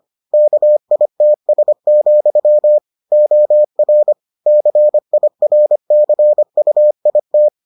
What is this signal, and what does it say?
Morse code, heard as a keyed tone, about 25 wpm, 600 Hz. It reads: KITS, OR CIRCUIT